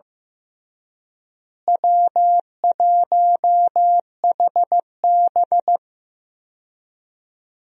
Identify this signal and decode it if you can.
Morse code, heard as a keyed tone, about 15 wpm, 700 Hz. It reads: W1HB
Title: USA Call Signs - 4 Characters - Repeated 3x - 15wpm